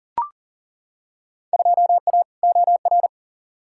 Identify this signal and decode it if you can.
Morse code, heard as a keyed tone, about 40 wpm, 700 Hz. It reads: 2A OR
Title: Field Day Exchanges - 40wpm